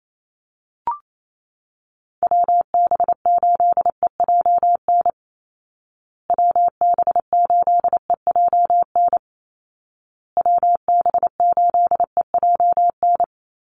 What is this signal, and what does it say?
Morse code, heard as a keyed tone, about 28 wpm, 700 Hz. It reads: WB8EJN WB8EJN WB8EJN
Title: Call Signs Repeated 2x - 28wpm